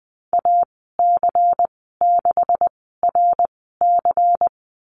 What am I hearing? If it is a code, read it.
Morse code, heard as a keyed tone, about 19 wpm, spaced as for 14 wpm, 700 Hz. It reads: AC6RC
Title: 128 Lesson 21g - Number 6 - Call Signs 20wpm